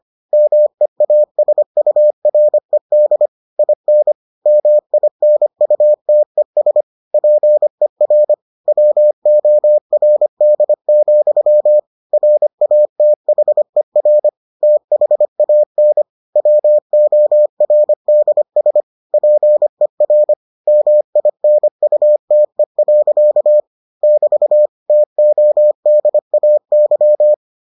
Morse code, heard as a keyed tone, about 25 wpm, 600 Hz. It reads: MEASURED IN MINUTES PER WORD, RATHER THAN WORDS PER MINUTE. <BT> TODAY